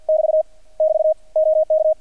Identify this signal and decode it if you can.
Morse code, heard as a keyed tone, about 50 wpm, 630 Hz. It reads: <BT> <BT> OK
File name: archived / CRSnet